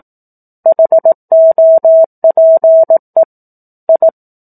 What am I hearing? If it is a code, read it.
Morse code, heard as a keyed tone, about 18 wpm, 650 Hz. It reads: HOPE I
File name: 04-Chapter-04 18 WPM